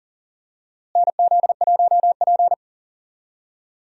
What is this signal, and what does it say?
Morse code, heard as a keyed tone, about 40 wpm, 700 Hz. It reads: NZ1P
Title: USA Call Signs - 4 Characters - Repeated 3x - 40wpm